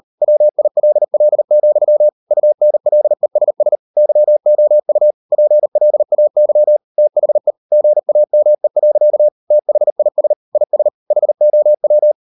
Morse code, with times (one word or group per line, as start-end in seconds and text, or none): 0.2-2.1 WILL,
2.3-3.8 UNLESS
4.0-5.1 YOU
5.3-6.8 PLAY
7.0-7.5 THE
7.7-9.3 GAME.
9.5-10.3 THIS
10.6-10.9 IS
11.1-12.1 HOW